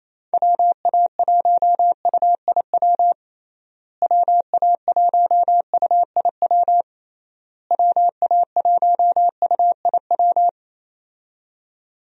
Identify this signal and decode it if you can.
Morse code, heard as a keyed tone, about 28 wpm, 700 Hz. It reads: WA1UIW WA1UIW WA1UIW
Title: USA Call Signs - 6 Characters - Repeated 3x - 28wpm